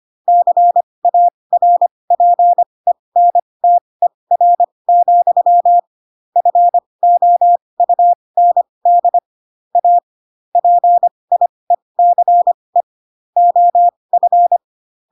Morse code, heard as a keyed tone, about 25 wpm, spaced as for 20 wpm, 700 Hz. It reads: CARPENTER, FOUND A PIECE OF
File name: Pinocchio / Pinocchio ch1 15 25 ik1hge